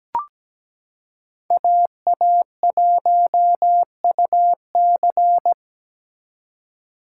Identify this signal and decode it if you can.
Morse code, heard as a keyed tone, about 17 wpm, 700 Hz. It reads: AA1UC
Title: K1USN SST QSO Practice - 17wpm